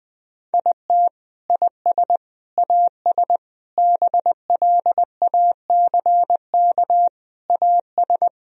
Morse code, heard as a keyed tone, about 21 wpm, 700 Hz. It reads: IT IS AS BLACK AS
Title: Sentences from Top 1000 Words - Review Part 2 - 20wpm